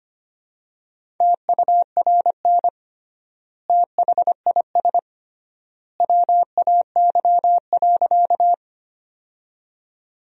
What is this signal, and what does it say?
Morse code, heard as a keyed tone, about 25 wpm, 700 Hz. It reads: TURN THIS WAY.
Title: 3x Word-Spacing - Sentences from Top 200 Words - 25wpm